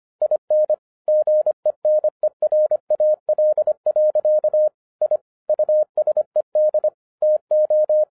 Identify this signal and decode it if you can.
Morse code, heard as a keyed tone, about 25 wpm, 610 Hz. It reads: IN GENERAL. I USED TO